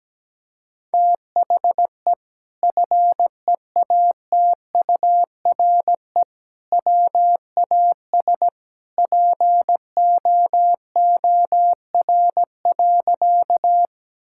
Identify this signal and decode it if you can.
Morse code, heard as a keyed tone, about 17 wpm, 700 Hz. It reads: THE FEATURE WAS POOR.
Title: Sentences from Top 1000 Words - Review Part 5 - 17wpm